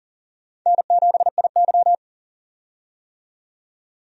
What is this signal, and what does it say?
Morse code, heard as a keyed tone, about 40 wpm, 700 Hz. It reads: N7IY
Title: USA Call Signs - 4 Characters - 40wpm